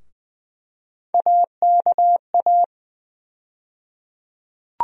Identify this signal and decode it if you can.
Morse code, heard as a keyed tone, about 20 wpm, 700 Hz. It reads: AKA
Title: USA Call Sign Postfixes - 3 Letters 20wpm